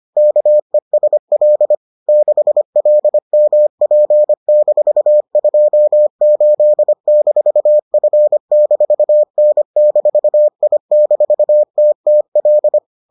Morse code, heard as a keyed tone, about 25 wpm, 600 Hz. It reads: KESL 6LMP-28-F-N-I-TTL